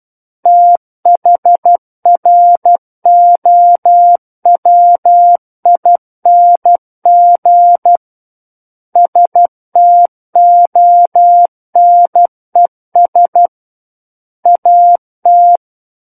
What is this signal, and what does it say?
Morse code, heard as a keyed tone, about 12 wpm, 700 Hz. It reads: THROWING STONES AT
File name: Book One / 03-Chapter 3 - 12 WPM